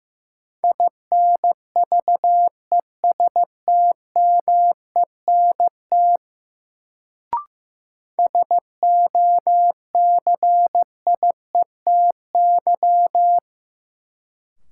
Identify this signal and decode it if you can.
Morse code, heard as a keyed tone, about 15 wpm, 700 Hz. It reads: INVESTMENT SOCIETY
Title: Sentences from Top 700 Words - New Word Drill - 15wpm